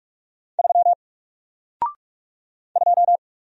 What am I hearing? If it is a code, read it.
Morse code, heard as a keyed tone, about 48 wpm, 700 Hz. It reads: 3 2
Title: Single Character (Letter:Number:Punctuation:Prosign) Alt 45wpm